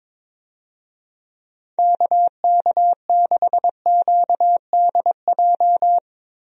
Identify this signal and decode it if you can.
Morse code, heard as a keyed tone, about 22 wpm, 700 Hz. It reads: KK6QDJ
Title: USA Call Signs - 6 Characters - Repeated 3x - 22wpm